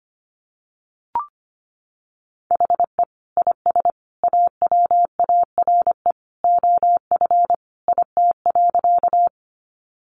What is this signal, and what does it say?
Morse code, heard as a keyed tone, about 25 wpm, 700 Hz. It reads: HE IS AWARE OF IT.
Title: Sentences from Top 1200 Words 25wpm